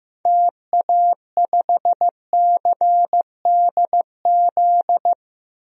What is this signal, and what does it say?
Morse code, heard as a keyed tone, about 15 wpm, 700 Hz. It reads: TA5CDZ
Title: DXCC-100 Call Signs - 6 Characters - Repeated 2x - 15wpm